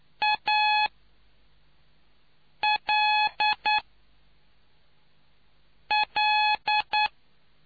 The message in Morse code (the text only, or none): ALL